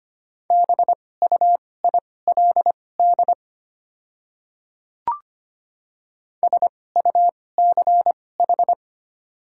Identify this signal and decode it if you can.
Morse code, heard as a keyed tone, about 25 wpm, spaced as for 18 wpm, 690 Hz. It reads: BUILD SUCH
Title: 2x Character Spacing - Top 500 Words - 25wpm